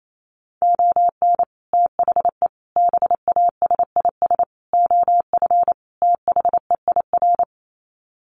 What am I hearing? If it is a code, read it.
Morse code, heard as a keyed tone, about 28 wpm, 700 Hz. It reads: ON THE BASIS OF THEIR
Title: Sets of 5 Words Encore 28wpm